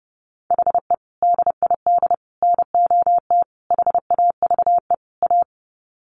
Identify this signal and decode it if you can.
Morse code, heard as a keyed tone, about 30 wpm, 700 Hz. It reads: HE DID NOT HAVE A
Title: Sets of 5 Words Encore 30wpm